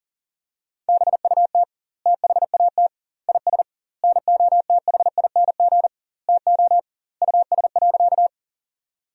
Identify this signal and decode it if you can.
Morse code, heard as a keyed tone, about 40 wpm, 700 Hz. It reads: BUT THAT IS NOTHING TO US.
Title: 2x Word-Spacing - Sentences from Top 500 Words - 40wpm